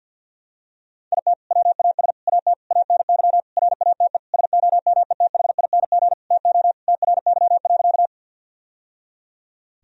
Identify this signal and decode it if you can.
Morse code, heard as a keyed tone, about 50 wpm, 700 Hz. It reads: IT WAS AT ANY RATE SOMETHING TO TRY.
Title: Sentences from Top 1000 Words - Review Part 4 - 50wpm